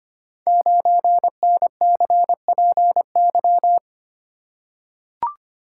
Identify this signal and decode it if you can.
Morse code, heard as a keyed tone, about 26 wpm, 700 Hz. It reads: 9NCPY
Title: Code Groups - 5 Letter:Numbers - 25wpm